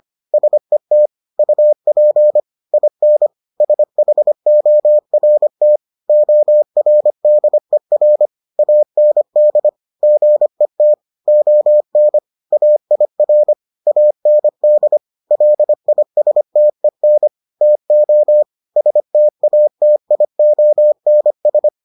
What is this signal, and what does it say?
Morse code, heard as a keyed tone, about 25 wpm, 600 Hz. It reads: SET UP IN SHORT ORDER AND GET ON AIR AND LISTEN TO STATIONS